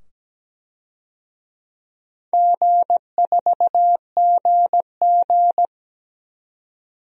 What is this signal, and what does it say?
Morse code, heard as a keyed tone, about 17 wpm, 700 Hz. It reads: G4GG
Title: DXCC-100 Call Signs - 4 Characters - 17wpm